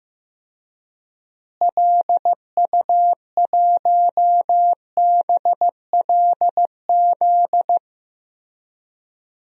Morse code, heard as a keyed tone, about 15 wpm, 700 Hz. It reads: LU1BLZ
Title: DXCC-100 Call Signs - 6 Characters - Repeated 3x - 15wpm